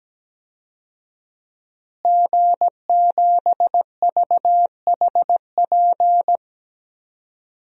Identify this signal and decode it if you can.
Morse code, heard as a keyed tone, about 17 wpm, 700 Hz. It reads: G7VHP
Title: DXCC-100 Call Signs - 5 Characters - Repeated 2x - 17wpm